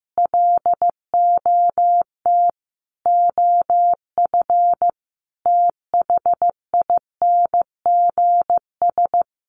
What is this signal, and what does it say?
Morse code, heard as a keyed tone, about 15 wpm, 700 Hz. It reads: LOT OF THINGS
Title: Sets of 4 Words 15wpm